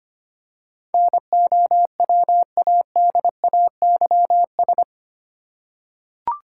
Morse code, heard as a keyed tone, about 25 wpm, 700 Hz. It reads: NOWADAYS
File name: Eight Letter Words - 25wpm